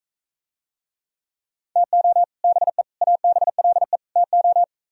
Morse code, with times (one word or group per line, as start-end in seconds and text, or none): 1.8-2.2 TO
2.4-2.8 BE
3.0-4.0 ABLE
4.2-4.6 TO